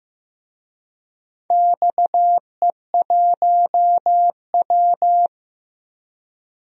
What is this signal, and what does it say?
Morse code, heard as a keyed tone, about 15 wpm, 700 Hz. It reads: XE1W
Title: DXCC-100 Call Signs - 4 Characters - Repeated 2x - 15wpm